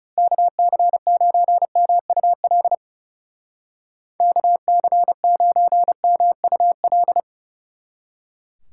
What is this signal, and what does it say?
Morse code, heard as a keyed tone, about 34 wpm, 700 Hz. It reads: KC9MUL KC9MUL
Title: Speed Racing - USA 2x3 Call Signs 30wpm